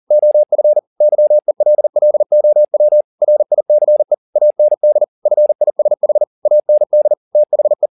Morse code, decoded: OF YELLOW RICE AND FISH AND THE

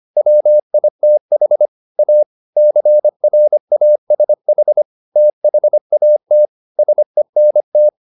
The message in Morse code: WITH A CRASH THAT SENT

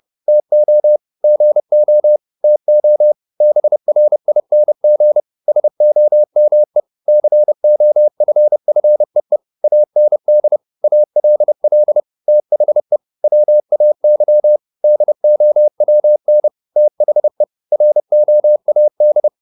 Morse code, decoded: TO GO TO BRING SOME COFFEE AND ALL THE WAY DOWN THE ROAD

